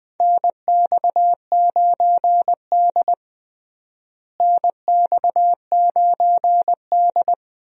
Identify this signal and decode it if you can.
Morse code, heard as a keyed tone, about 20 wpm, 700 Hz. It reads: NX9D NX9D
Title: USA Call Signs - 4 Characters - Repeated 2x - 20wpm